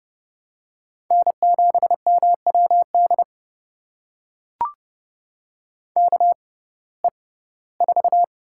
Morse code, heard as a keyed tone, about 30 wpm, spaced as for 18 wpm, 700 Hz. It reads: N7MWD KE4
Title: USA Call Signs - 5 Characters - 6x Character Spacing - 30wpm